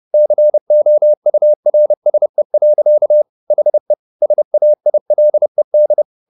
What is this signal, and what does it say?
Morse code, heard as a keyed tone, about 30 wpm, 600 Hz. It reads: COURSE. HE SAILED